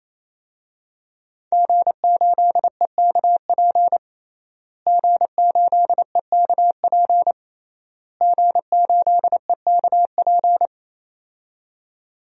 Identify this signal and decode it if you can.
Morse code, heard as a keyed tone, about 28 wpm, 700 Hz. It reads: G8EKP G8EKP G8EKP
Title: DXCC-100 Call Signs - 5 Characters - Repeated 3x - 28wpm